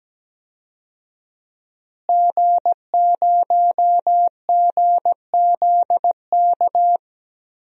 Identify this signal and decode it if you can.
Morse code, heard as a keyed tone, about 17 wpm, 700 Hz. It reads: G0GZK
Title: DXCC-100 Call Signs - 5 Characters - Repeated 3x - 17wpm